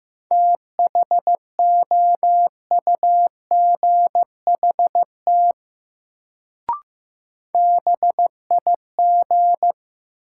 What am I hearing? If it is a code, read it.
Morse code, heard as a keyed tone, about 15 wpm, 700 Hz. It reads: THOUGHT BIG
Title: Sets of 2 Words - New Word Drill - 15wpm